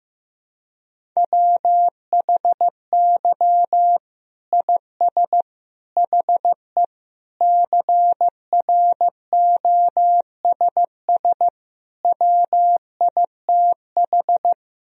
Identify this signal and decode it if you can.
Morse code, heard as a keyed tone, about 15 wpm, 700 Hz. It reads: WHY IS HE CROSS WITH
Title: Sentences from Top 1000 Words - Review Part 1 - 15wpm